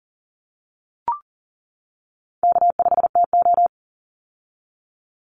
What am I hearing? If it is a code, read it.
Morse code, heard as a keyed tone, about 40 wpm, 700 Hz. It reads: K5TO
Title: Easy Call Signs - 40wpm